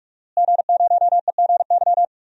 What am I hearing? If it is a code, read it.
Morse code, heard as a keyed tone, about 46 wpm, 700 Hz. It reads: G0EGY
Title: DXCC-100 Call Signs - 5 Characters - Repeated 3x - 45wpm